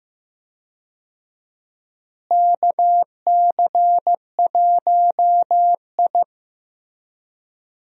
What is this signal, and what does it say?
Morse code, heard as a keyed tone, about 15 wpm, 700 Hz. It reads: KC1I
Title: USA Call Signs - 4 Characters - Repeated 3x - 15wpm